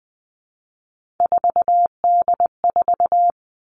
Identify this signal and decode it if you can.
Morse code, heard as a keyed tone, about 20 wpm, 700 Hz. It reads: 4D4
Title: All Cut Numbers - 3-Digit 20wpm